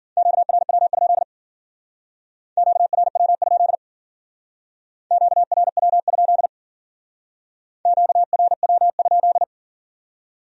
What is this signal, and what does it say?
Morse code, heard as a keyed tone, about 49 wpm, 700 Hz. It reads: QRW? QRW? QRW? QRW?
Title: Speed-Racing Q-Codes - 40wpm